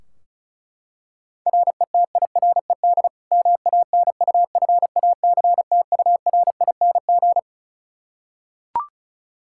Morse code, heard as a keyed tone, about 34 wpm, 700 Hz. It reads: RETIRED MANUFACTURING
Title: K1USN SST QSO Practice 35wpm